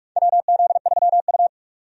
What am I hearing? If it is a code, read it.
Morse code, heard as a keyed tone, about 45 wpm, 700 Hz. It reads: WZ3U
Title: USA Call Signs - 4 Characters - 45wpm